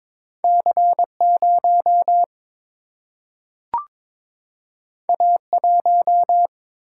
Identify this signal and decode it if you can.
Morse code, heard as a keyed tone, about 23 wpm, 700 Hz. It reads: C0 A1
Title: Single-Letter Single-Number 22wpm